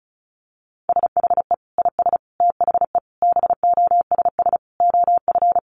Morse code, heard as a keyed tone, about 34 wpm, 700 Hz. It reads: SHE IS THE BOSS OF
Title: Sentences from Top 700 Words 35wpm